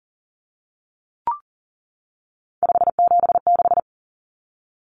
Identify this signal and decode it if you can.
Morse code, heard as a keyed tone, about 40 wpm, 700 Hz. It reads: 576
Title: Common Cut Numbers - 3-Digit 40wpm